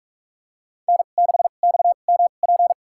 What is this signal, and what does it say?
Morse code, heard as a keyed tone, about 46 wpm, spaced as for 33 wpm, 700 Hz. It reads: N6XMP
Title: USA Call Signs - 5 Characters - 2x Character Spacing - 45wpm